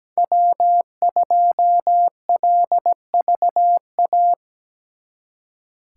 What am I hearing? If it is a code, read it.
Morse code, heard as a keyed tone, about 20 wpm, 700 Hz. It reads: W2LVA
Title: Speed Racing - USA 1x3 Call Signs 17wpm